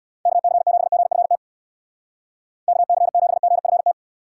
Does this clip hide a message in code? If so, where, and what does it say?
Yes: Morse code from 0.3 to 3.9 s, heard as a keyed tone, about 64 wpm, 700 Hz. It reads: KC7GFT KC7GFT